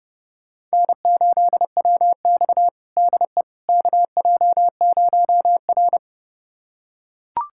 Morse code, heard as a keyed tone, about 30 wpm, 700 Hz. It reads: N8WX DE KJ0R
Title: USA Callsign DE Callsign - 4 Characters - 30wpm